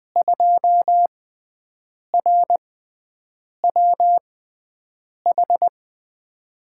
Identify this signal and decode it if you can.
Morse code, heard as a keyed tone, about 21 wpm, 700 Hz. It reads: 2 R W H
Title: USA Call Signs - 6x Character Spacing - 20wpm